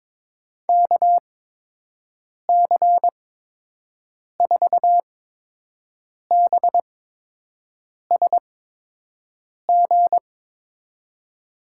Morse code, heard as a keyed tone, about 23 wpm, 700 Hz. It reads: K C 4 B S G
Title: USA Call Signs - 6 Characters - 8x Character Spacing - 22wpm